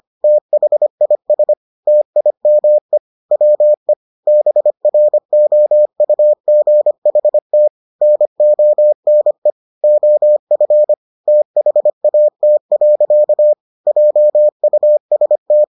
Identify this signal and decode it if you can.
Morse code, heard as a keyed tone, about 25 wpm, 600 Hz. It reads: THIS TIME WE BROUGHT NONE OF THAT. JUST